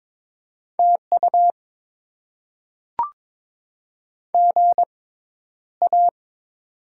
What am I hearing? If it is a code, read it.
Morse code, heard as a keyed tone, about 22 wpm, spaced as for 11 wpm, 690 Hz. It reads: TU GA
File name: Top 20 QSO Elements - 6x Character Spacing - 22wpm